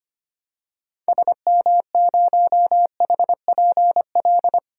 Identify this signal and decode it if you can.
Morse code, heard as a keyed tone, about 25 wpm, 700 Hz. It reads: SM0HPL
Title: DXCC-100 Call Signs - 6 Characters - 25wpm